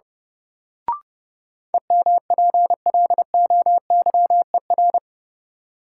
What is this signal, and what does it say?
Morse code, heard as a keyed tone, about 30 wpm, 700 Hz. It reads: EMPLOYER
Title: Sentences from Top 1600 Words - New Word Drill - 30wpm